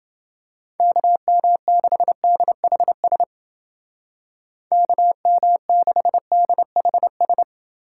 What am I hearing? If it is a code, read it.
Morse code, heard as a keyed tone, about 29 wpm, 700 Hz. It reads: KM6DHS KM6DHS